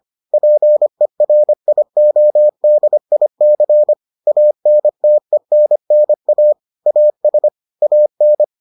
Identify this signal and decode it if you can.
Morse code, heard as a keyed tone, about 25 wpm, 600 Hz. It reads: PERIODIC ANTENNA AS AN